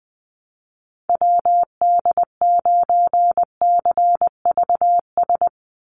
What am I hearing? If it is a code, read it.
Morse code, heard as a keyed tone, about 20 wpm, 700 Hz. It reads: WD9CVS
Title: USA Call Signs 20wpm